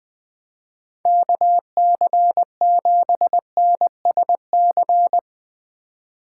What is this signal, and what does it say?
Morse code, heard as a keyed tone, about 21 wpm, 700 Hz. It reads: KC7NSC
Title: Speed Racing - USA 2x3 Call Signs 20wpm